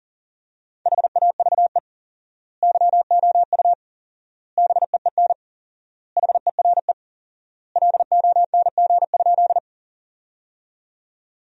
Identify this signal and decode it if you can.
Morse code, heard as a keyed tone, about 40 wpm, 700 Hz. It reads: HAVE YOU BEEN HERE LONG?
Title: 4x Word-Spacing - Sentences from Top 200 Words - 40wpm